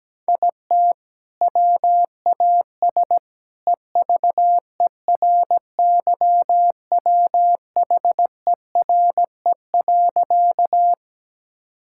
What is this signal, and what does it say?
Morse code, heard as a keyed tone, about 17 wpm, 700 Hz. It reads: IT WAS EVERYWHERE.
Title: Sentences from Top 1600 Words 17wpm